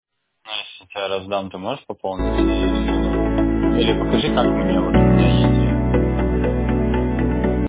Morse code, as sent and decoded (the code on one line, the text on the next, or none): none